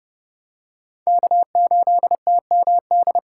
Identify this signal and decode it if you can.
Morse code, heard as a keyed tone, about 29 wpm, 700 Hz. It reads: K8TMD